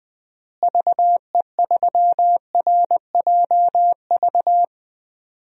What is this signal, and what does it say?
Morse code, heard as a keyed tone, about 20 wpm, 700 Hz. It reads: VE3RJV
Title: DXCC-100 Call Signs - 6 Characters - Repeated 2x - 20wpm